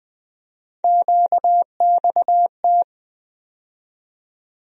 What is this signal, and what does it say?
Morse code, heard as a keyed tone, about 20 wpm, 700 Hz. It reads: QXT
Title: USA Call Sign Postfixes - 3 Letters 20wpm